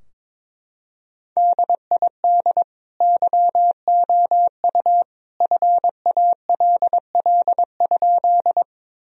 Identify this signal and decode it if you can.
Morse code, heard as a keyed tone, about 22 wpm, 700 Hz. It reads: DID YOU FALL?
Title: Sentences from Top 1000 Words - Review Part 2 - 22wpm